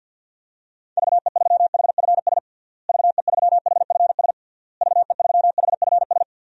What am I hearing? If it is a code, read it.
Morse code, heard as a keyed tone, about 50 wpm, 700 Hz. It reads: VE3HFS VE3HFS VE3HFS
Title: DXCC-100 Call Signs - 6 Characters - Repeated 3x - 50wpm